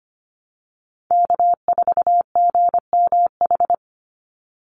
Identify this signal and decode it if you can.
Morse code, heard as a keyed tone, about 25 wpm, 700 Hz. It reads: K4GMH
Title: Call Signs Repeated 2x - 25wpm